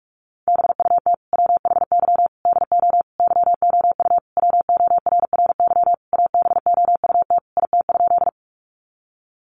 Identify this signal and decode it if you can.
Morse code, heard as a keyed tone, about 44 wpm, 700 Hz. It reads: BUT WHY DO YOU WORRY ABOUT IT?
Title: Sentences from Top 1100 Words 45wpm